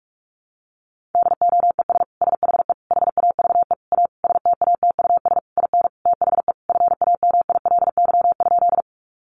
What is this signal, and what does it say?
Morse code, heard as a keyed tone, about 44 wpm, 700 Hz. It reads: DOES SHE HAVE A STATUS IN THE FAMILY?
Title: Sentences from Top 1400 Words 45wpm